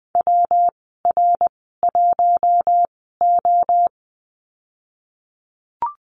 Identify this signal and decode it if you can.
Morse code, heard as a keyed tone, about 20 wpm, spaced as for 14 wpm, 700 Hz. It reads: WR1O
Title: 193 Lesson 31g - Letter v - Call Signs 20wpm